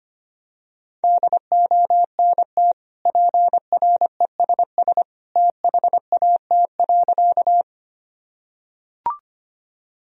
Morse code, heard as a keyed tone, about 25 wpm, 700 Hz. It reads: DONT PRESS THAT.
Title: Sentences from Top 1000 Words - Review Part 1 - 25wpm